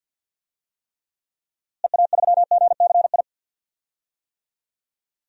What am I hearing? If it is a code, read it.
Morse code, heard as a keyed tone, about 53 wpm, 700 Hz. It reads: EA3GCI
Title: DXCC-100 Call Signs - 6 Characters - Repeated 2x - 50wpm